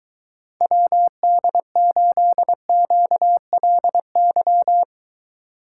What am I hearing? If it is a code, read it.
Morse code, heard as a keyed tone, about 23 wpm, 700 Hz. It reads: WD8QLY